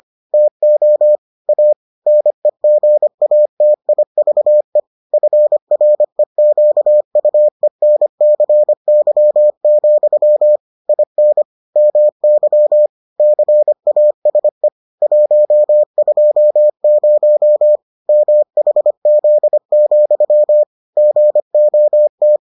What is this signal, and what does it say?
Morse code, heard as a keyed tone, about 25 wpm, 600 Hz. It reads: TO A NEGATIVE FREQUENCY, IN MY CASE 120 MHZ, GOT